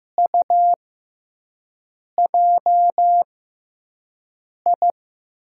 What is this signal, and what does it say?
Morse code, heard as a keyed tone, about 15 wpm, spaced as for 5 wpm, 700 Hz. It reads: UJI
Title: USA Call Signs - 6x Character Spacing - 15wpm